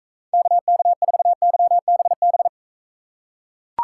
Morse code, heard as a keyed tone, about 43 wpm, 700 Hz. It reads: KK4YBB